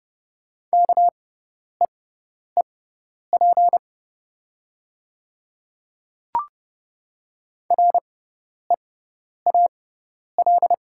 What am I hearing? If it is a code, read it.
Morse code, heard as a keyed tone, about 31 wpm, spaced as for 10 wpm, 700 Hz. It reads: KEEP REAL
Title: 6x Character Spacing - Top 300 Words - 30wpm